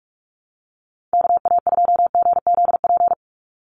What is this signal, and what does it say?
Morse code, heard as a keyed tone, about 44 wpm, 700 Hz. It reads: KA2GZP
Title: USA Call Signs 45wpm